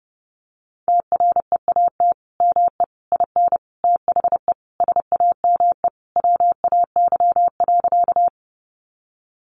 Morse code, heard as a keyed tone, about 28 wpm, 700 Hz. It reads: TREAT ME IN THE SAME WAY.
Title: Sentences from Top 1300 Words 30wpm